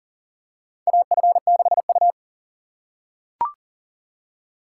Morse code, heard as a keyed tone, about 40 wpm, 700 Hz. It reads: AF6U
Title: USA Call Signs - 4 Characters - 2x Character Spacing - 40wpm